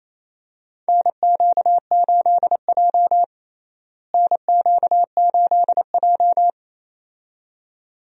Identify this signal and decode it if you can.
Morse code, heard as a keyed tone, about 28 wpm, 700 Hz. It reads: NQ8J NQ8J
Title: USA Call Signs - 4 Characters - Repeated 2x - 28wpm